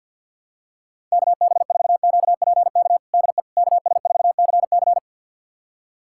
Morse code, heard as a keyed tone, about 52 wpm, 690 Hz. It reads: KB4QPK DE KS4CC
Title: USA Callsign DE Callsign 50wpm